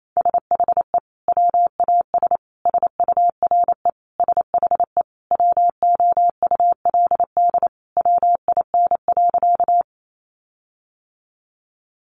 Morse code, heard as a keyed tone, about 28 wpm, 700 Hz. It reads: SHE WAS SURE SHE WOULD WIN.
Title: Sentences from Top 1500 Words 28wpm